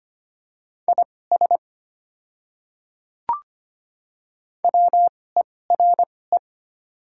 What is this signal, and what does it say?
Morse code, heard as a keyed tone, about 25 wpm, spaced as for 18 wpm, 700 Hz. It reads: IS WERE
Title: 2x Character Spacing - Top 100 Words - 25wpm